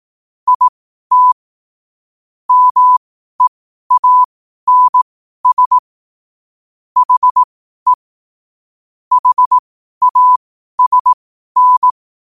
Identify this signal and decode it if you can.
Morse code, heard as a keyed tone, about 18 wpm, spaced as for 12 wpm, 1000 Hz. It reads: IT MEANS HE HASN
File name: qotdfiles / 12wpm